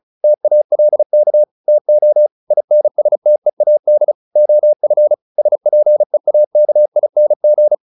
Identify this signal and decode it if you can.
Morse code, heard as a keyed tone, about 34 wpm, 600 Hz. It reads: TALK TO INSTEAD OF SPEAKING
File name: the old man and the sea 2 35wpm